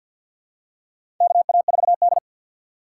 Morse code, heard as a keyed tone, about 53 wpm, 700 Hz. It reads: KA4D
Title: USA Call Signs - 3x Character Spacing - 50wpm